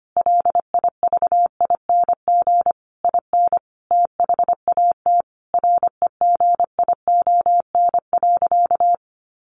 Morse code, heard as a keyed tone, about 25 wpm, 700 Hz. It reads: LIVING IN THAT REGION.
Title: Sentences from Top 1500 Words 25wpm